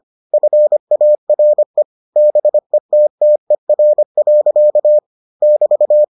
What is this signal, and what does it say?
Morse code, heard as a keyed tone, about 25 wpm, 600 Hz. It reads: FARE BETTER. <BT>